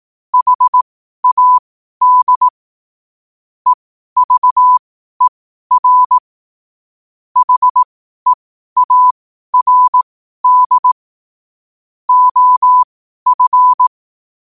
Morse code, heard as a keyed tone, about 18 wpm, spaced as for 12 wpm, 1000 Hz. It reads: HAD EVER HEARD OF
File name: qotdfiles / 12wpm